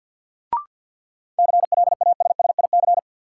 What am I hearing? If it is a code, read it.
Morse code, heard as a keyed tone, about 52 wpm, 700 Hz. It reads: CLASSIC
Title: Sentences from Top 1000 Words - New Word Drill - 50wpm